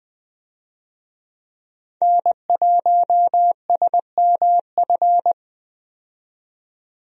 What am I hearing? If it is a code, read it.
Morse code, heard as a keyed tone, about 20 wpm, 700 Hz. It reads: N1SMF
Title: USA Call Signs - 5 Characters - 20wpm